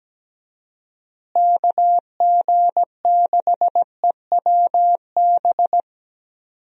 Morse code, heard as a keyed tone, about 17 wpm, 700 Hz. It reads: KG6EWB
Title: USA Call Signs - 6 Characters - Repeated 3x - 17wpm